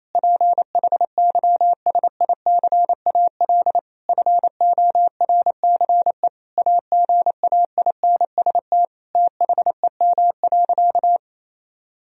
Morse code, heard as a keyed tone, about 28 wpm, 700 Hz. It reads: PHYSICAL FORCE AGAINST THEM.
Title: Sentences from Top 1000 Words - Review Part 4 - 28wpm